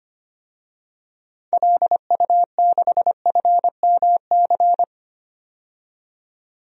Morse code, heard as a keyed tone, about 25 wpm, 700 Hz. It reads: LU6FMC
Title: DXCC-100 Call Signs - 6 Characters - 25wpm